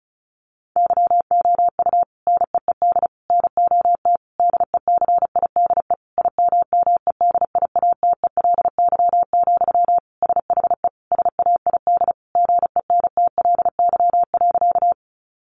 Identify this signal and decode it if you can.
Morse code, heard as a keyed tone, about 34 wpm, 700 Hz. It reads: YOU NEED NOT DECIDE IMMEDIATELY, SHE SAID GENTLY.
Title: Sentences from Top 1500 Words 35wpm